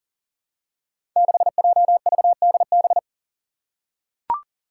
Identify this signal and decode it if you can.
Morse code, heard as a keyed tone, about 40 wpm, 700 Hz. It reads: 6JVDB
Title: Five Challenging Letter-Number Combinations 40wpm